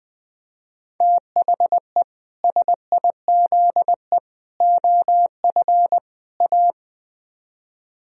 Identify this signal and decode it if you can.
Morse code, heard as a keyed tone, about 20 wpm, 700 Hz. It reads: THE SIZE OF A
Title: Speed-Racing - Sets of 4 Words 17wpm